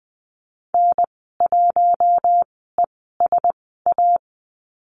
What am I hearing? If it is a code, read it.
Morse code, heard as a keyed tone, about 20 wpm, spaced as for 15 wpm, 700 Hz. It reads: N1ESA
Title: 055 Lesson 10g - Letter r - Call Signs 20wpm